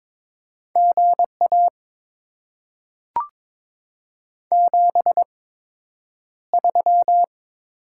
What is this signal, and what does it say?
Morse code, heard as a keyed tone, about 22 wpm, spaced as for 9 wpm, 700 Hz. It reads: GA 73
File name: Top 20 QSO Elements - 8x Character Spacing - 22wpm